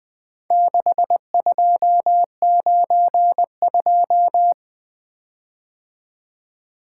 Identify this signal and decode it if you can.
Morse code, heard as a keyed tone, about 21 wpm, 700 Hz. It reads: 6292
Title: Speed-Racing - Four Letter-Number Combinations - 20wpm